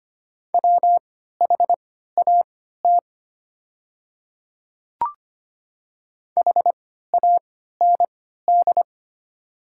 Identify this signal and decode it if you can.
Morse code, heard as a keyed tone, about 25 wpm, spaced as for 14 wpm, 700 Hz. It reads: WHAT HAND
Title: 3x Character Spacing - Top 200 Words - 25wpm